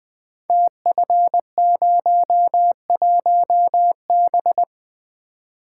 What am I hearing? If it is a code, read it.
Morse code, heard as a keyed tone, about 21 wpm, 700 Hz. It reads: TF01B
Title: Code Groups - 5 Letter:Numbers - 20wpm